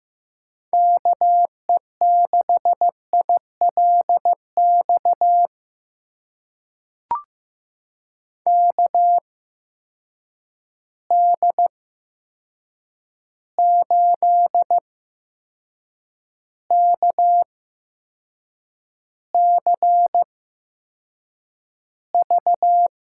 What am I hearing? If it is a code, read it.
Morse code, heard as a keyed tone, about 15 wpm, spaced as for 6 wpm, 700 Hz. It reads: KE6ILX KD8KCV
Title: USA Call Signs - 6 Characters - 8x Character Spacing - 15wpm